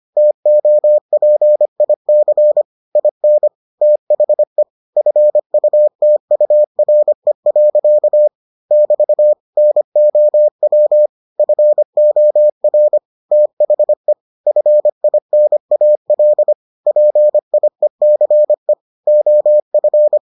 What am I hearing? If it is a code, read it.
Morse code, heard as a keyed tone, about 25 wpm, 600 Hz. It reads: TOPIC IN THE FUTURE. <BT> NOW FOR THE FINAL PIECE OF